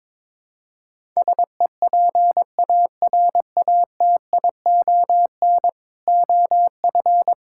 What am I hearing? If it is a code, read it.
Morse code, heard as a keyed tone, about 22 wpm, 700 Hz. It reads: SEPARATION OF
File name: practice / 22wpm / Build A Phrase - 5-Word Phrases - 22wpm